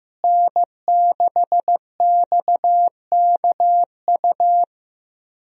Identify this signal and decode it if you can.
Morse code, heard as a keyed tone, about 15 wpm, 700 Hz. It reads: N6XKU
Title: USA Call Signs - 5 Characters - Repeated 3x - 15wpm